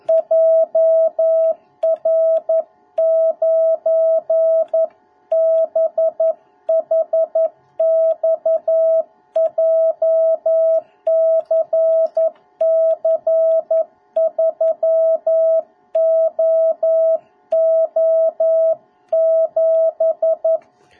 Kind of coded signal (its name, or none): Morse code